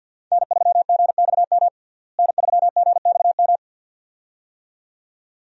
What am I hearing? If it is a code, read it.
Morse code, heard as a keyed tone, about 53 wpm, 700 Hz. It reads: N3GXM N3GXM
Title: USA Call Signs - 5 Characters - Repeated 2x - 50wpm